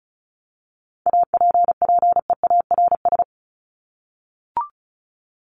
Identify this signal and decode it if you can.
Morse code, heard as a keyed tone, about 34 wpm, 700 Hz. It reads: APPEARS
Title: ES:S Suffix - Top 100 Words with Base Words 35wpm